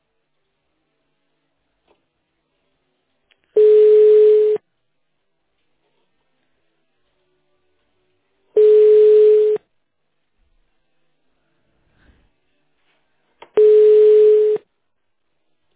Ringback tone (Europe, 425 Hz).
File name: 1012. .6770f958-90ed-413b-b89a